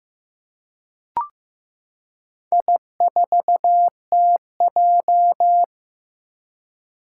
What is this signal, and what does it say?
Morse code, heard as a keyed tone, about 15 wpm, 700 Hz. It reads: I4TJ
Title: Four Challenging Letter-Number Combinations 15wpm